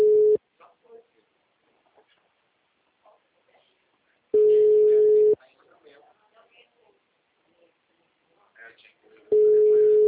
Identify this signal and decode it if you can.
Ringback tone (Europe, 425 Hz), with speech.
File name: out- -2004- - - .1458